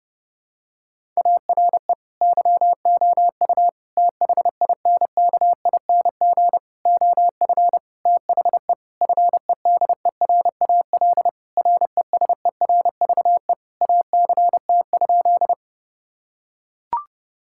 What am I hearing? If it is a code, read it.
Morse code, heard as a keyed tone, about 31 wpm, 700 Hz. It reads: ARE YOU THINKING OF THE FEDERAL RESERVE ACT?
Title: Sentences from Top 1000 Words - Review Part 6 - 30wpm